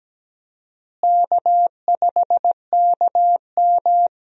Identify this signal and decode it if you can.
Morse code, heard as a keyed tone, about 17 wpm, 700 Hz. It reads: K5KM
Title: USA Call Signs - 4 Characters - Repeated 3x - 17wpm